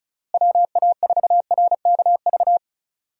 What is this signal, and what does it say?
Morse code, heard as a keyed tone, about 36 wpm, 700 Hz. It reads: WA4RKV
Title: USA Call Signs - 6 Characters - Repeated 3x - 35wpm